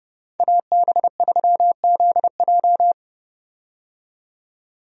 Morse code, heard as a keyed tone, about 30 wpm, 700 Hz. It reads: AB3ZJ
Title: USA Call Signs - 5 Characters - 30wpm